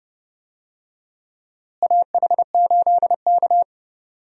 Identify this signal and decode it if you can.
Morse code, heard as a keyed tone, about 31 wpm, 700 Hz. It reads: AH8K
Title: USA Call Signs - 4 Characters - Repeated 3x - 30wpm